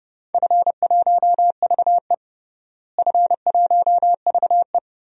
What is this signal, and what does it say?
Morse code, heard as a keyed tone, about 32 wpm, 700 Hz. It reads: F1VE F1VE
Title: DXCC-100 Call Signs - 4 Characters - Repeated 2x - 30wpm